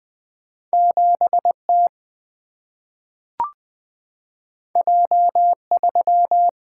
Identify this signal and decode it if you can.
Morse code, heard as a keyed tone, about 20 wpm, 700 Hz. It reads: 7 T J 3
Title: Single-Letter Single-Number 20wpm